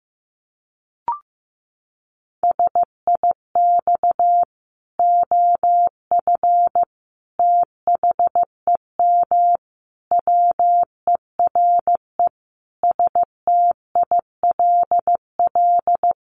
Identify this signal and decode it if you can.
Morse code, heard as a keyed tone, about 15 wpm, 700 Hz. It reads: SIX OF THEM WERE STILL
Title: Sentences from Top 1100 Words 15wpm